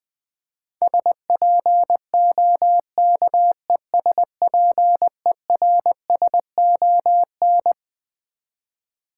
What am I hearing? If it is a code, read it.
Morse code, heard as a keyed tone, about 20 wpm, 700 Hz. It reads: SPOKESPERSON
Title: Build A Word - 6 Letters At A Time - 12-Letter Words - 20wpm